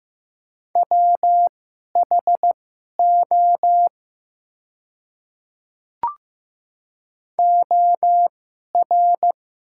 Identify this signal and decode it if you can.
Morse code, heard as a keyed tone, about 15 wpm, spaced as for 11 wpm, 700 Hz. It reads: WHO OR
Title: 2x Character Spacing - Top 100 Words - 15wpm